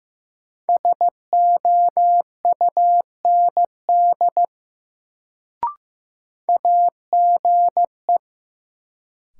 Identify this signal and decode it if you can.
Morse code, heard as a keyed tone, about 15 wpm, 700 Hz. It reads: SOUND AGE
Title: Sets of 3 Words - New Word Drill - 15wpm